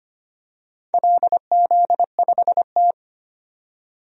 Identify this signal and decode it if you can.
Morse code, heard as a keyed tone, about 25 wpm, 700 Hz. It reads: LZ5T
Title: DXCC-100 Call Signs - 4 Characters - 25wpm